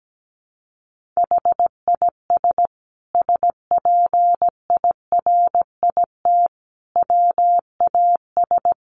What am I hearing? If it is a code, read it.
Morse code, heard as a keyed tone, about 17 wpm, 700 Hz. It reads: HIS SPIRIT WAS
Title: Sentences from Top 1400 Words 17wpm